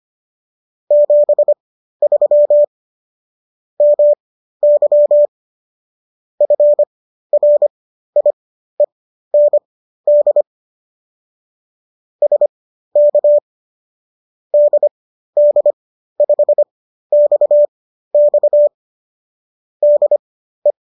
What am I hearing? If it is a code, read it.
Morse code, heard as a keyed tone, about 25 wpm, spaced as for 13 wpm, 590 Hz. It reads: 73 MY FRIEND SK DD5XX DE